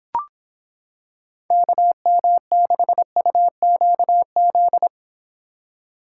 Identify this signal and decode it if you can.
Morse code, heard as a keyed tone, about 26 wpm, 700 Hz. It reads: KM6UQZ